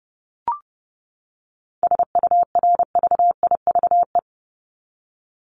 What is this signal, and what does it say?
Morse code, heard as a keyed tone, about 30 wpm, 700 Hz. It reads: SURVIVE
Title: Top 1251-1500 Words 30wpm